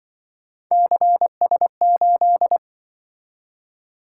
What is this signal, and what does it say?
Morse code, heard as a keyed tone, about 24 wpm, 700 Hz. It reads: CS8
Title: Speed-Racing - Three Letter-Number Combinations - 20wpm